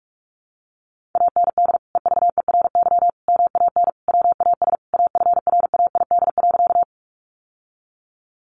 Morse code, heard as a keyed tone, about 44 wpm, 700 Hz. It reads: AND EVERY MAN WAS AFRAID.
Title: Sentences from Top 1500 Words 45wpm